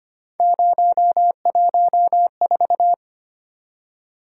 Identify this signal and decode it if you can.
Morse code, heard as a keyed tone, about 25 wpm, 700 Hz. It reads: 014
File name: Three Letter-Number Combinations - 25wpm